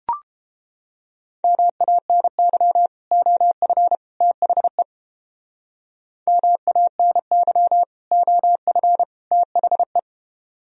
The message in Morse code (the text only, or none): MANY OF THE MANY OF THE